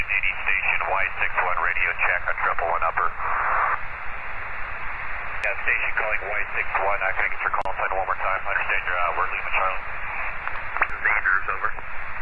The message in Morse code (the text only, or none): none